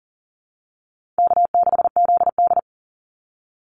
Morse code, heard as a keyed tone, about 40 wpm, 700 Hz. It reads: K6ZD
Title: Call Signs Repeated 2x - 40wpm